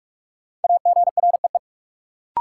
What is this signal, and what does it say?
Morse code, heard as a keyed tone, about 44 wpm, 700 Hz. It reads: AGREE
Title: Sentences from Top 1000 Words - Review Part 6 - New Word Drill - 45wpm